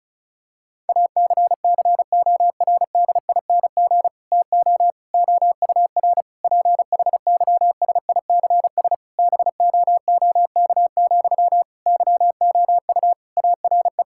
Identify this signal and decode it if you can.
Morse code, heard as a keyed tone, about 36 wpm, 700 Hz. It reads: ACCORDING TO OUR PHYSICS BOOK, YOU ARE
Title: Sentences from Top 1000 Words - Review Part 5 - 35wpm